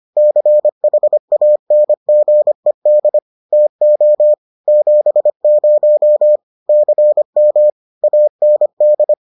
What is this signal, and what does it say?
Morse code, heard as a keyed tone, about 25 wpm, 600 Hz. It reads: CHANGED TO 70 CM AND